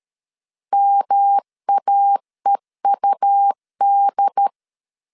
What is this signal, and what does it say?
Morse code, heard as a keyed tone, about 13 wpm, 800 Hz. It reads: MAEUD
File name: CW / DET / 1 3 1500C S C 12wpm 2750